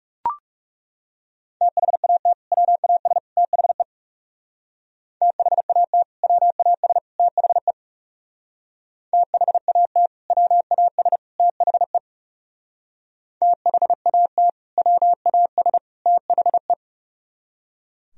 Morse code, heard as a keyed tone, about 38 wpm, 700 Hz. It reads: THAT WAS THE THAT WAS THE THAT WAS THE THAT WAS THE